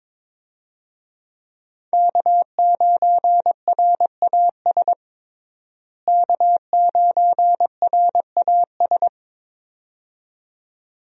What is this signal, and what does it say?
Morse code, heard as a keyed tone, about 22 wpm, 700 Hz. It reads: K9RAS K9RAS
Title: USA Call Signs - 5 Characters - Repeated 2x - 22wpm